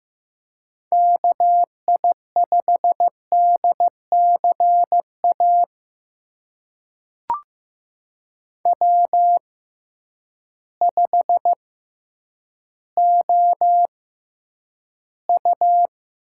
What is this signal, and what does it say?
Morse code, heard as a keyed tone, about 15 wpm, spaced as for 9 wpm, 700 Hz. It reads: KI5DCA W5OU
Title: USA Call Signs - 6x Character Spacing - 15wpm